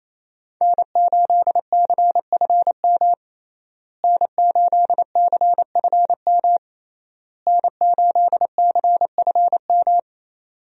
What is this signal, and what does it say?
Morse code, heard as a keyed tone, about 28 wpm, 700 Hz. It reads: N8CFM N8CFM N8CFM
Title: USA Call Signs - 5 Characters - Repeated 3x - 28wpm